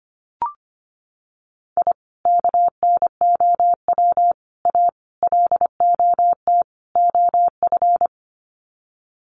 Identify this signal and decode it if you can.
Morse code, heard as a keyed tone, about 25 wpm, 700 Hz. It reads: I KNOW A LOT OF
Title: Sets of 5 Words Encore 25wpm